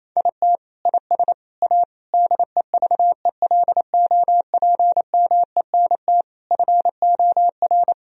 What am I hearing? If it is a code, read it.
Morse code, heard as a keyed tone, about 28 wpm, 700 Hz. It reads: IT IS A DEVELOPMENT FOR
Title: Sentences from Top 1000 Words - Review Part 4 - 28wpm